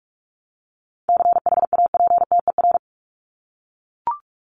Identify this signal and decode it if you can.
Morse code, heard as a keyed tone, about 44 wpm, 700 Hz. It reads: CHAPTER
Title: Top 1751-2000 Words 45wpm